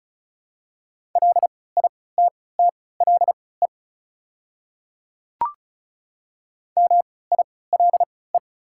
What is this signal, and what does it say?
Morse code, heard as a keyed tone, about 36 wpm, spaced as for 20 wpm, 690 Hz. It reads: LITTLE MILE
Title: 3x Character Spacing - Top 300 Words - 35wpm